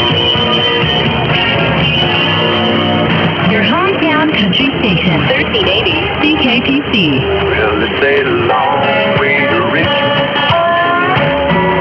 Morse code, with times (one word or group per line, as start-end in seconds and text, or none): none